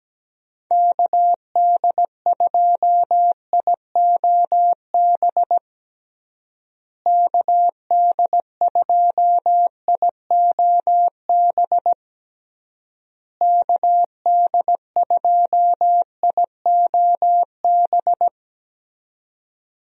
Morse code, heard as a keyed tone, about 17 wpm, 700 Hz. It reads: KD2IOB KD2IOB KD2IOB
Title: USA Call Signs - 6 Characters - Repeated 3x - 17wpm